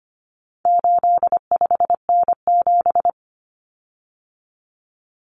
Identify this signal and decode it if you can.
Morse code, heard as a keyed tone, about 25 wpm, 700 Hz. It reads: 85N7
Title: Common Cut Numbers - 4-Digit 25wpm